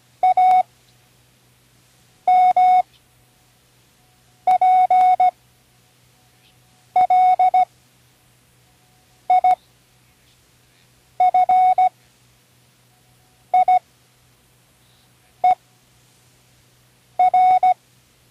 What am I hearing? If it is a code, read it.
Morse code, heard as a keyed tone, about 16 wpm, spaced as for 5 wpm, 740 Hz. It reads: AMPLIFIER
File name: Learning Morse Code (from Tune in the World)